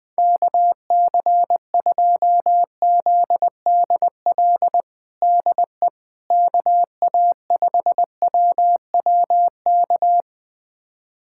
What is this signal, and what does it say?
Morse code, heard as a keyed tone, about 19 wpm, 700 Hz. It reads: KC2ZDL DE KA5WWK
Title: USA Callsign DE Callsign 20wpm